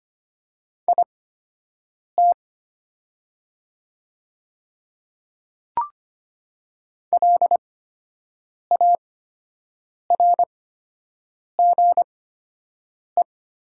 Morse code, heard as a keyed tone, about 25 wpm, spaced as for 7 wpm, 700 Hz. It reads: IT LARGE
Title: 8x Character Spacing - Top 200 Words - 25wpm